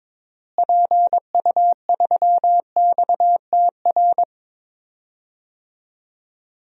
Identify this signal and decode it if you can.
Morse code, heard as a keyed tone, about 22 wpm, 700 Hz. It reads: PU3XTR
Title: DXCC-100 Call Signs - 6 Characters - Repeated 2x - 22wpm